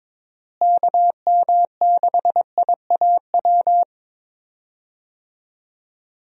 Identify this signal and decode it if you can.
Morse code, heard as a keyed tone, about 22 wpm, 700 Hz. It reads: KM6IAW
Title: USA Call Signs - 6 Characters - Repeated 2x - 22wpm